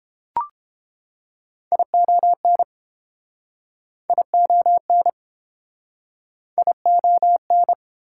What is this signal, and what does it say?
Morse code, heard as a keyed tone, about 28 wpm, 690 Hz. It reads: ION ION ION